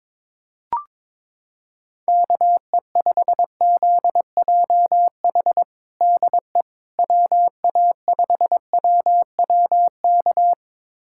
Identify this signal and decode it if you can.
Morse code, heard as a keyed tone, about 22 wpm, 700 Hz. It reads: KE5ZJH DE WA5WWK
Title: USA Callsign DE Callsign - 6 Characters - 22wpm